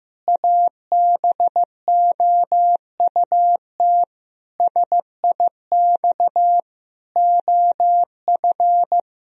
Morse code, heard as a keyed tone, about 15 wpm, 700 Hz. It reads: ABOUT SIX OF
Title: Sentences from Top 1000 Words - Review Part 2 - 15wpm